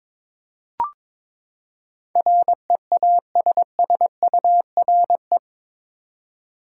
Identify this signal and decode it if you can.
Morse code, heard as a keyed tone, about 22 wpm, 700 Hz. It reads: REASSURE
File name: practice / 22wpm / Eight Letter Words - 22wpm